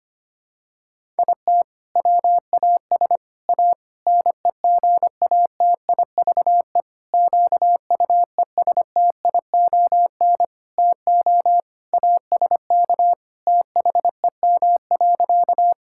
Morse code, heard as a keyed tone, about 25 wpm, 700 Hz. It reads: IT WAS A NEGATIVE QUESTION TO ASK THEM.
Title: Sentences from Top 1000 Words - Review Part 6 - 25wpm